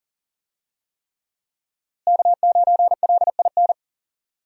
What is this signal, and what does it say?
Morse code, heard as a keyed tone, about 40 wpm, 700 Hz. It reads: K9LIN
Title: USA Call Signs - 5 Characters - Repeated 3x - 40wpm